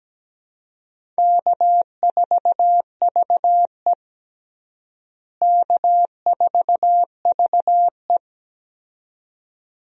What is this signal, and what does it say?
Morse code, heard as a keyed tone, about 17 wpm, 700 Hz. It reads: K4VE K4VE
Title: USA Call Signs - 4 Characters - Repeated 2x - 17wpm